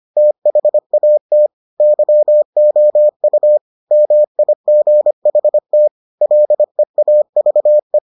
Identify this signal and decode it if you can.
Morse code, heard as a keyed tone, about 25 wpm, 600 Hz. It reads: THAT YOU MIGHT LEAVE